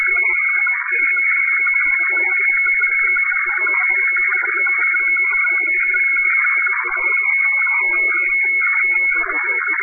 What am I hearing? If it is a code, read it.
Weather fax (radiofax).
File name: hfwefax / 12788.1.khz. .2025